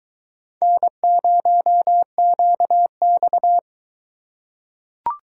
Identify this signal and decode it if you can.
Morse code, heard as a keyed tone, about 20 wpm, 700 Hz. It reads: N0QX